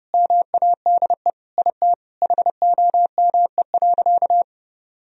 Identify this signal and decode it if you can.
Morse code, heard as a keyed tone, about 30 wpm, 700 Hz. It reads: MADE IT HOME.
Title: Sentences from Top 1000 Words - Review Part 2 - 30wpm